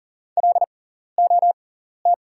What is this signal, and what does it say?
Morse code, heard as a keyed tone, about 40 wpm, spaced as for 14 wpm, 690 Hz. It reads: LOT
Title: 6x Character Spacing - Top 500 Words - 40wpm